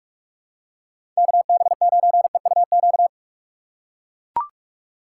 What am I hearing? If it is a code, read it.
Morse code, heard as a keyed tone, about 43 wpm, 700 Hz. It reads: KB9EUQ
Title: USA Call Signs - 6 Characters - 3x Character Spacing - 45wpm